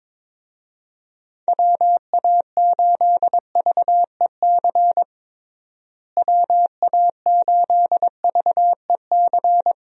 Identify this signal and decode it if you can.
Morse code, heard as a keyed tone, about 22 wpm, 700 Hz. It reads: WA8VEC WA8VEC
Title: USA Call Signs - 6 Characters - Repeated 2x - 22wpm